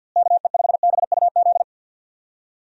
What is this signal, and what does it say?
Morse code, heard as a keyed tone, about 48 wpm, 700 Hz. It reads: KE5BUZ
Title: USA Call Signs - 6 Characters - Repeated 2x - 50wpm